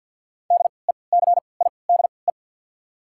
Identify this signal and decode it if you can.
Morse code, heard as a keyed tone, about 49 wpm, spaced as for 28 wpm, 690 Hz. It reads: DECIDE
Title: 3x Character Spacing - Top 500 Words - 50wpm